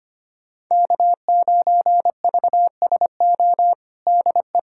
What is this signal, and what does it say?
Morse code, heard as a keyed tone, about 25 wpm, 700 Hz. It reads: K9VSO DE
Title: USA Callsign DE Callsign - 5 Characters - 25wpm